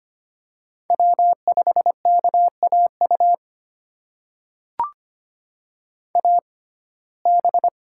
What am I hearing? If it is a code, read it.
Morse code, heard as a keyed tone, about 25 wpm, spaced as for 18 wpm, 700 Hz. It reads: W5KAU AB
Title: USA Call Signs - 6x Character Spacing - 25wpm